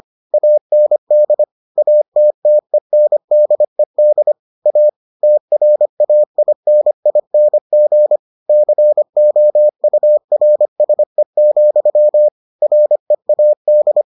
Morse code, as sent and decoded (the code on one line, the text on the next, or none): .- -. -.. / .- - - . -. -.. -.. / .- / - .-. .- .. -. .. -. --. / -.-. --- ..- .-. ... --..-- / .-. .- -..
AND ATTENDED A TRAINING COURSE, READ